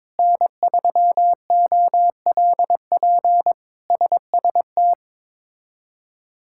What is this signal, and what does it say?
Morse code, heard as a keyed tone, about 22 wpm, 700 Hz. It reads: N3OLP SST
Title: K1USN SST QSO Practice - 22wpm